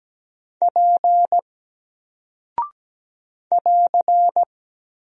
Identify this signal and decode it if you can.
Morse code, heard as a keyed tone, about 17 wpm, 700 Hz. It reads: P <AR>
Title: Single Character (Letter:Number:Punctuation:Prosign) 17wpm